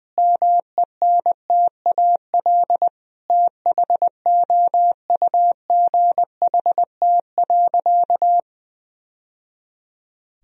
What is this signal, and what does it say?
Morse code, heard as a keyed tone, about 20 wpm, 700 Hz. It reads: MENTAL THOUGHT.
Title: Sentences from Top 1000 Words - Review Part 4 - 20wpm